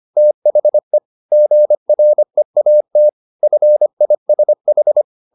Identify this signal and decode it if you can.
Morse code, heard as a keyed tone, about 25 wpm, 600 Hz. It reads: THE GREAT FISH
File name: the old man and the sea part 1 25wpm